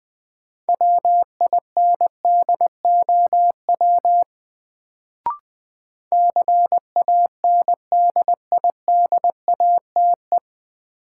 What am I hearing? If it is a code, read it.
Morse code, heard as a keyed tone, about 20 wpm, 700 Hz. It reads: WINDOW CANDIDATE
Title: Sentences from Top 1600 Words - New Word Drill - 20wpm